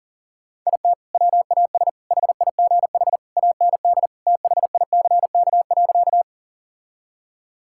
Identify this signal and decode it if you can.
Morse code, heard as a keyed tone, about 37 wpm, 700 Hz. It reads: IT WAS HIGH AND THICK.
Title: Sentences from Top 1600 Words 40wpm